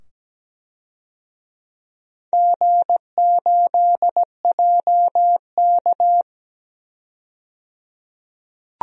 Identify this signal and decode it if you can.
Morse code, heard as a keyed tone, about 17 wpm, 700 Hz. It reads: G8JK
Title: DXCC-100 Call Signs - 4 Characters - 17wpm